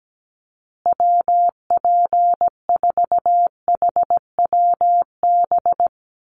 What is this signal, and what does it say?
Morse code, heard as a keyed tone, about 17 wpm, 700 Hz. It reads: WP4HWB
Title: USA Call Signs Repeated 2x 17wpm